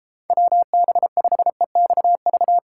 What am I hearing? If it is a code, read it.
Morse code, heard as a keyed tone, about 36 wpm, 700 Hz. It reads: WB5EXV